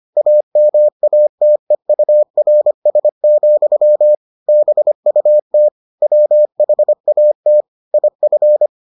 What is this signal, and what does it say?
Morse code, heard as a keyed tone, about 25 wpm, 600 Hz. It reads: AMATEURS, BUT WHAT IF